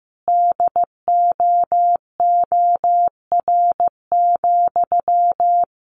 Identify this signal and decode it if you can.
Morse code, heard as a keyed tone, about 15 wpm, 700 Hz. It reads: DOOR,
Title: Sentences from Top 1400 Words 15wpm